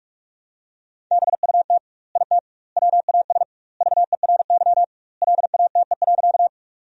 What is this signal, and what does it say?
Morse code, heard as a keyed tone, about 46 wpm, 700 Hz. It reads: BUT IT WAS VERY LATE.
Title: 2x Word-Spacing - Sentences from Top 300 Words - 45wpm